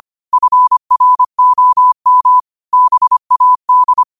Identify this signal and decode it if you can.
Morse code, heard as a keyed tone, about 25 wpm, 1000 Hz. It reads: FROM BAD